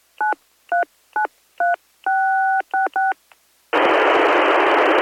DTMF, heard as telephone touch tones, tones of about 150 ms. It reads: 9393666